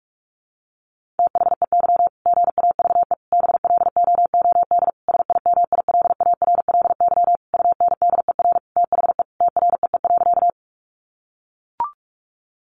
Morse code, heard as a keyed tone, about 44 wpm, 700 Hz. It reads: THEY GAVE BLOOD SIMILARLY UNDER THE TREE.
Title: Sentences from Top 1400 Words 45wpm